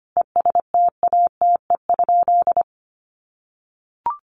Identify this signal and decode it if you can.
Morse code, heard as a keyed tone, about 25 wpm, 700 Hz. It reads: ESTATE?
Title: Sentences from Top 1200 Words 25wpm